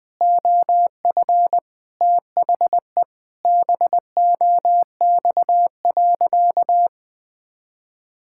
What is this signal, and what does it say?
Morse code, heard as a keyed tone, about 20 wpm, 700 Hz. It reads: OF THE BOX.
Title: Sentences with 5-Word Phrases - Spoken Beginning - 20wpm